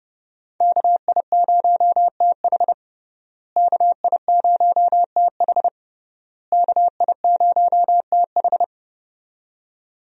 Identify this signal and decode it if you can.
Morse code, heard as a keyed tone, about 32 wpm, 700 Hz. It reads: KI0TH KI0TH KI0TH
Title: USA Call Signs - 5 Characters - Repeated 3x - 30wpm